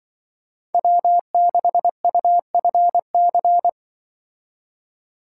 Morse code, heard as a keyed tone, about 23 wpm, 700 Hz. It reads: W6UFC
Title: Speed Racing - USA 1x3 Call Signs 20wpm